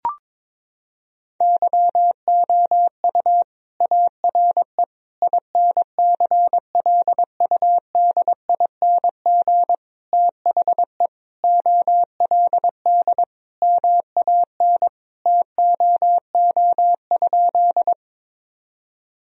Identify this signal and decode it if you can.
Morse code, heard as a keyed tone, about 21 wpm, 700 Hz. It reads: YOU ARE INCLUDING THE OLD MAN TOO?
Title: Sentences from Top 1000 Words - Review Part 4 - 22wpm